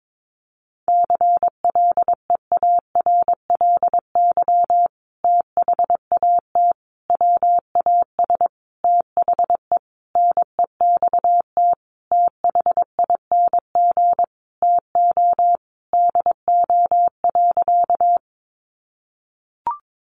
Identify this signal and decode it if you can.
Morse code, heard as a keyed tone, about 22 wpm, 700 Hz. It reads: CLEARLY THAT WAS THE NEXT THING TO DO.
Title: Sentences from Top 1300 Words 22wpm